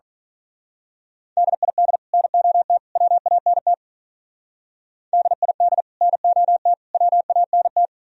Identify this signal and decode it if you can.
Morse code, heard as a keyed tone, about 41 wpm, 700 Hz. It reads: DID NOT WANT DID NOT WANT
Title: Speed-Racing - Sets of 3 Words 35wpm